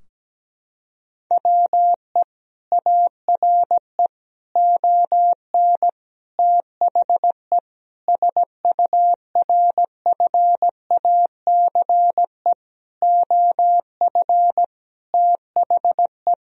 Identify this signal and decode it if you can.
Morse code, heard as a keyed tone, about 17 wpm, 700 Hz. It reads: WE ARE ON THE SURFACE OF THE
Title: Sentences from Top 1000 Words - Review Part 3 - 17wpm